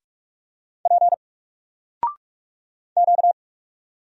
Morse code, heard as a keyed tone, about 46 wpm, 700 Hz. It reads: P Q
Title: Single Character (Letter:Number:Punctuation:Prosign) 45wpm